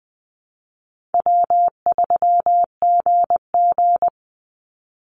Morse code, heard as a keyed tone, about 20 wpm, 700 Hz. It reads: W3GG
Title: Easy Call Signs - 20wpm